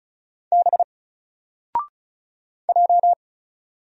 Morse code, heard as a keyed tone, about 36 wpm, 700 Hz. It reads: B J
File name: Single-Letter-Number-35wpm